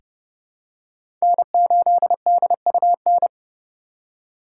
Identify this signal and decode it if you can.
Morse code, heard as a keyed tone, about 29 wpm, 700 Hz. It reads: N8DUN
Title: USA Call Signs - 5 Characters - 2x Character Spacing - 30wpm